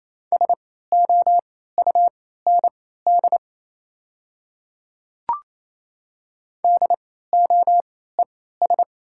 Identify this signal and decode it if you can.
Morse code, heard as a keyed tone, about 29 wpm, spaced as for 16 wpm, 690 Hz. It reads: SOUND DOES
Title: 3x Character Spacing - Top 200 Words - 28wpm